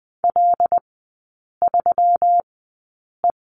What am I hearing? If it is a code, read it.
Morse code, heard as a keyed tone, about 20 wpm, spaced as for 8 wpm, 700 Hz. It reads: L3E
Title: 121 Lesson 20e - Number 3 - Three Characters 20wpm